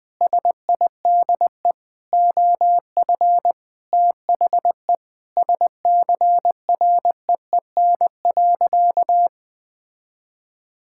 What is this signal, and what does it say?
Morse code, heard as a keyed tone, about 19 wpm, 700 Hz. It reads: SIDE OF THE SCREEN.
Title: Sentences with 6-Word Phrases - 20wpm